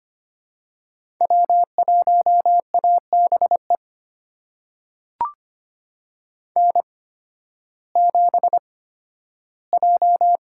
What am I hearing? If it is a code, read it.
Morse code, heard as a keyed tone, about 25 wpm, spaced as for 13 wpm, 700 Hz. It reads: W1ABE N7J
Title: USA Call Signs - 8x Character Spacing - 25wpm